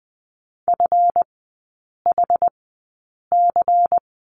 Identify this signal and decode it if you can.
Morse code, heard as a keyed tone, about 20 wpm, spaced as for 8 wpm, 700 Hz. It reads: FHC
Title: 138 Lesson 23e - Letter f - Three Characters 20wpm